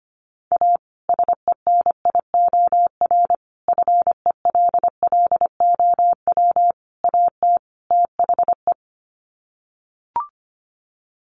Morse code, heard as a keyed tone, about 25 wpm, 700 Hz. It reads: A SENIOR FELLOW AT THE
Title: Sets of 5 Words Encore 25wpm